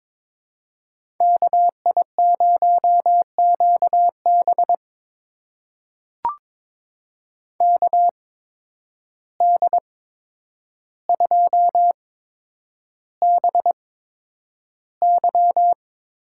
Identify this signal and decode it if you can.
Morse code, heard as a keyed tone, about 23 wpm, spaced as for 10 wpm, 700 Hz. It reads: KI0QB KD2BY
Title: USA Call Signs - 8x Character Spacing - 22wpm